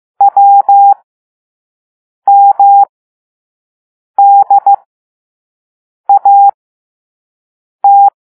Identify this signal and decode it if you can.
Morse code, heard as a keyed tone, about 15 wpm, spaced as for 5 wpm, 800 Hz. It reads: WMDAT
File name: Morse / Level6-All-A